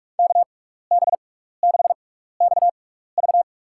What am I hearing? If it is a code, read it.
Morse code, heard as a keyed tone, about 45 wpm, spaced as for 15 wpm, 700 Hz. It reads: KB6XV
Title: USA Call Signs - 5 Characters - 6x Character Spacing - 45wpm